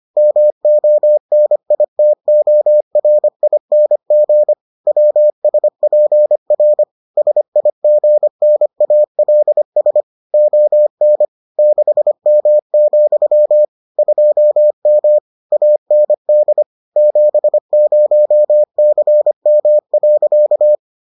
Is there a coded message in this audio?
Yes: Morse code from 0.2 to 20.8 s, heard as a keyed tone, about 25 wpm, 600 Hz. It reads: MONITORING WSPR SIGNALS ON 6M, 2M AND 70CM.